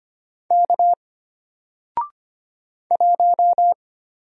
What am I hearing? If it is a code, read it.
Morse code, heard as a keyed tone, about 25 wpm, 700 Hz. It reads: K 1
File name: Single-Letter-Number-25wpm